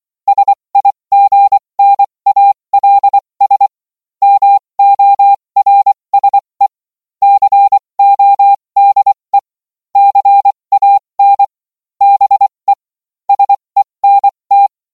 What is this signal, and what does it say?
Morse code, heard as a keyed tone, about 24 wpm, spaced as for 20 wpm, 800 Hz. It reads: SIGNALS MORSE CODE CAN BE SENT